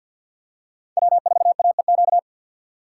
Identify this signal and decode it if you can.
Morse code, heard as a keyed tone, about 51 wpm, 690 Hz. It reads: W4AEQ
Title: USA Call Signs - 5 Characters - 6x Character Spacing - 50wpm